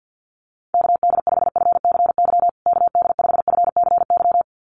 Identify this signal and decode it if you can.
Morse code, heard as a keyed tone, about 50 wpm, 700 Hz. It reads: KD5FCY KD5FCY
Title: USA Call Signs Repeated 2x 50wpm